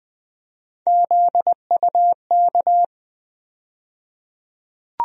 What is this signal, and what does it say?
Morse code, heard as a keyed tone, about 20 wpm, 700 Hz. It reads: ZUK
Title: USA Call Sign Postfixes - 3 Letters 20wpm